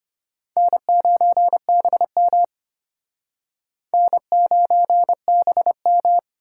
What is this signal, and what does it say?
Morse code, heard as a keyed tone, about 33 wpm, 700 Hz. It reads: N9BM N9BM